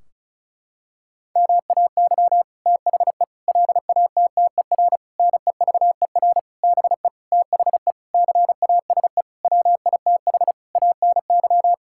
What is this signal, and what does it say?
Morse code, heard as a keyed tone, about 36 wpm, 700 Hz. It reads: MAY THE LATTER NEVER BE THE CASE WITH ANY
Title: Sentences from Top 1600 Words 35wpm